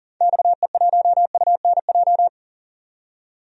Morse code, heard as a keyed tone, about 40 wpm, 700 Hz. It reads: XE1UNJ
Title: DXCC-100 Call Signs - 6 Characters - 40wpm